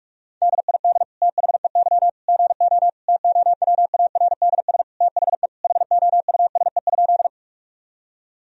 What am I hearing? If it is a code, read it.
Morse code, heard as a keyed tone, about 46 wpm, 700 Hz. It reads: DID THEY GO TOWARDS THE HOUSE?
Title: Sentences from Top 1000 Words - Review Part 2 - 45wpm